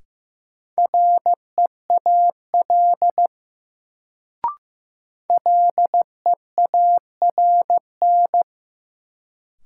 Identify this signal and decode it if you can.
Morse code, heard as a keyed tone, about 15 wpm, 700 Hz. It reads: REAL LEARN
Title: Sentences from Top 300 Words - New Word Drill - 15wpm